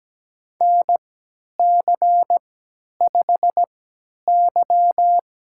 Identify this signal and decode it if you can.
Morse code, heard as a keyed tone, about 17 wpm, spaced as for 10 wpm, 700 Hz. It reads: NC5Y
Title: USA Call Signs - 4 Characters - 3x Character Spacing - 17wpm